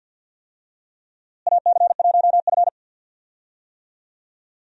Morse code, heard as a keyed tone, about 46 wpm, 700 Hz. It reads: AC1F
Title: USA Call Signs - 4 Characters - Repeated 2x - 50wpm